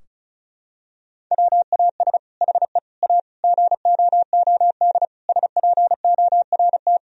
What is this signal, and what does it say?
Morse code, heard as a keyed tone, about 34 wpm, 700 Hz. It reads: WAS HE A GOOD SPORT
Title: Sentences from Top 1000 Words - Review Part 4 - 35wpm